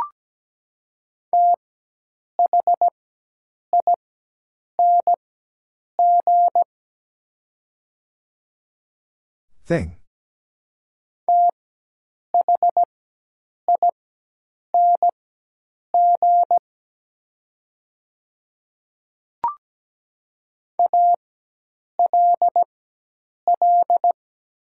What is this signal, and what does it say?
Morse code, heard as a keyed tone, about 17 wpm, spaced as for 8 wpm, 700 Hz. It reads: THING THING ALL
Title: 4x Character Spacing - Top 200 Words - 17wpm